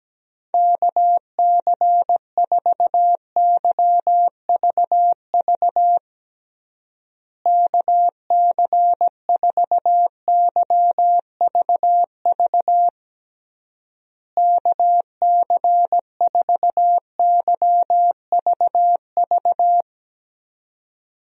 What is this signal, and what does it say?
Morse code, heard as a keyed tone, about 17 wpm, 700 Hz. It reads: KC4YVV KC4YVV KC4YVV
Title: USA Call Signs - 6 Characters - Repeated 3x - 17wpm